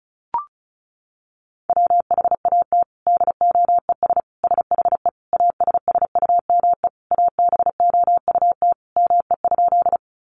Morse code, heard as a keyed tone, about 34 wpm, 700 Hz. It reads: WHAT DOES SHE ASSUME ABOUT ME?
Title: Sentences from Top 1500 Words 35wpm